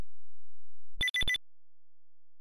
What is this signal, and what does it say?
There is no code.